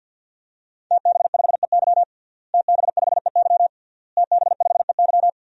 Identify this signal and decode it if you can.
Morse code, heard as a keyed tone, about 52 wpm, 700 Hz. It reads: TB5EY TB5EY TB5EY
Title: DXCC-100 Call Signs - 5 Characters - Repeated 3x - 50wpm